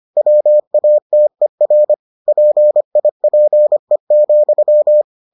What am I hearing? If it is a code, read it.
Morse code, heard as a keyed tone, about 25 wpm, 600 Hz. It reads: WATER PIPE,